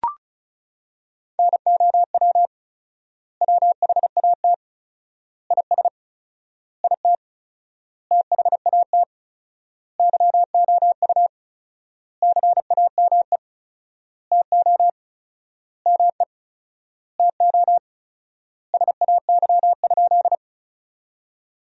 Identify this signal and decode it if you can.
Morse code, heard as a keyed tone, about 36 wpm, 700 Hz. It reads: NOW WHAT IS IT THAT YOU CAME TO ME TO SAY?
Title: 4x Word-Spacing - Sentences from Top 200 Words - 35wpm